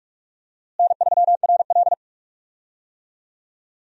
Morse code, heard as a keyed tone, about 41 wpm, 700 Hz. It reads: N3RL
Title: USA Call Signs - 4 Characters - Repeated 3x - 45wpm